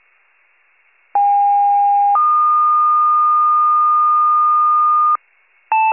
Two-tone page, heard as a one-second tone, then a three-second tone, about 810 then 1220 Hz. It is heard from 1.1 to 5.2 s.